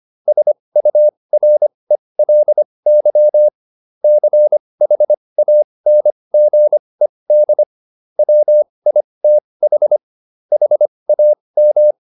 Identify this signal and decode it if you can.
Morse code, heard as a keyed tone, about 25 wpm, spaced as for 20 wpm, 600 Hz. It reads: SURELY CHANGED WITH HAM